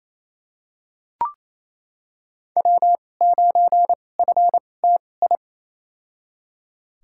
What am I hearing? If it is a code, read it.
Morse code, heard as a keyed tone, about 28 wpm, spaced as for 20 wpm, 700 Hz. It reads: W9FTI
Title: USA Call Signs - 5 Characters - 2x Character Spacing - 28wpm